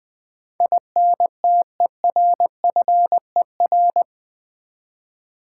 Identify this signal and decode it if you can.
Morse code, heard as a keyed tone, about 20 wpm, 700 Hz. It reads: INTERFER
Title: Build A Word - 4 Letters At A Time - 12-Letter Words - 20wpm